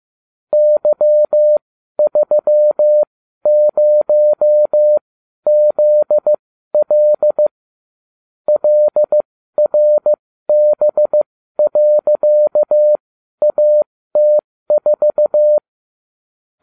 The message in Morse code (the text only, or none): Y30ZL LRB.AT4